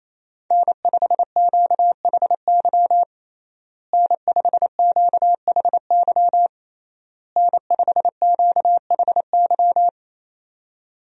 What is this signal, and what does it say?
Morse code, heard as a keyed tone, about 28 wpm, 700 Hz. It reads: N5QHY N5QHY N5QHY
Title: USA Call Signs - 5 Characters - Repeated 3x - 28wpm